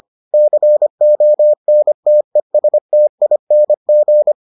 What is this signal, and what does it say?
Morse code, heard as a keyed tone, about 25 wpm, 600 Hz. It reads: CONTESTING